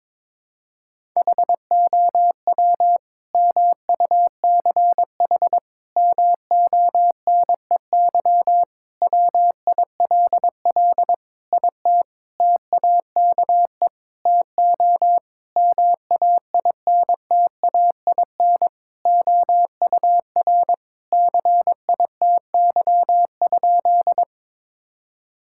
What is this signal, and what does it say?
Morse code, heard as a keyed tone, about 22 wpm, 700 Hz. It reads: HOW MUCH MONEY WILL IT TAKE TO MAINTAIN OUR CITY?
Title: Sentences from Top 1000 Words - Review Part 5 - 22wpm